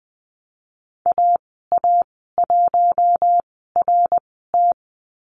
Morse code, heard as a keyed tone, about 20 wpm, spaced as for 15 wpm, 700 Hz. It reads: AA1RT
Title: 088 Lesson 15g - Number 5 - Call Signs 20wpm